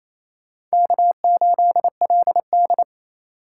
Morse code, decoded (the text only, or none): K8LD